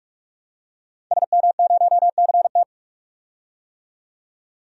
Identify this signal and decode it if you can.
Morse code, heard as a keyed tone, about 41 wpm, 700 Hz. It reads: SM0CT
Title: DXCC-100 Call Signs - 5 Characters - 45wpm